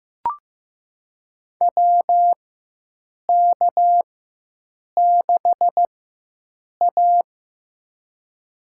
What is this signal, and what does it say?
Morse code, heard as a keyed tone, about 15 wpm, spaced as for 7 wpm, 700 Hz. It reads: WK6A
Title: USA Call Signs - 4 Characters - 4x Character Spacing - 15wpm